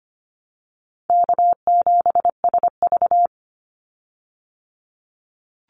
Morse code, heard as a keyed tone, about 25 wpm, 700 Hz. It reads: K7SV
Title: Easy Call Signs Repeated 2x - 25wpm